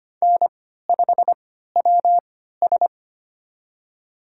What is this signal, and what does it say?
Morse code, heard as a keyed tone, about 25 wpm, spaced as for 14 wpm, 700 Hz. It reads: N5WS
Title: USA Call Signs - 4 Characters - 3x Character Spacing - 25wpm